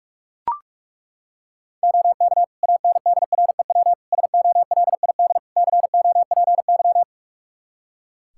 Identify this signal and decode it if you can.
Morse code, heard as a keyed tone, about 41 wpm, 700 Hz. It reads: OK ANDREW SOLID COPY
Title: Practice QSO - Exchange 3 - More Info - 45wpm